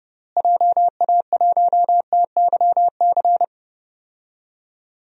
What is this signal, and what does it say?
Morse code, heard as a keyed tone, about 30 wpm, 700 Hz. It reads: JA1TYC
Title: DXCC-100 Call Signs - 6 Characters - Repeated 2x - 30wpm